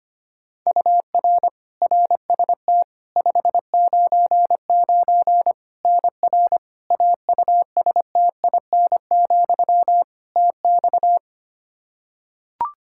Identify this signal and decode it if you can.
Morse code, heard as a keyed tone, about 25 wpm, 700 Hz. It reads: UR RST 599 NR AUSTIN, TX
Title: Practice QSO - Exchange 1 - Beginning - 25wpm